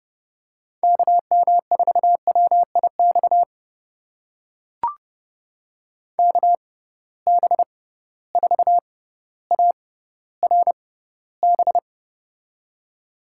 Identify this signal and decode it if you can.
Morse code, heard as a keyed tone, about 32 wpm, spaced as for 16 wpm, 700 Hz. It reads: KM4WIX KB4ARB
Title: USA Call Signs - 6 Characters - 6x Character Spacing - 30wpm